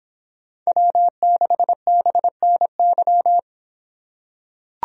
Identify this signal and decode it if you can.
Morse code, heard as a keyed tone, about 24 wpm, 700 Hz. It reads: W6BNY